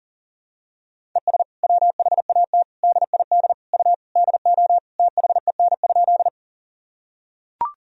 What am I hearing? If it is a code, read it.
Morse code, heard as a keyed tone, about 40 wpm, 700 Hz. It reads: ES WHAT DID U DO THEN?
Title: Sentences from Top 100 Words and Abbreviations 40wpm